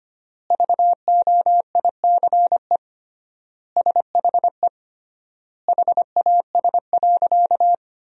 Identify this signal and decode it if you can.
Morse code, heard as a keyed tone, about 25 wpm, 700 Hz. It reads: VOICE SHE HAS.
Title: 3x Word-Spacing - Sentences from Top 500 Words - 25wpm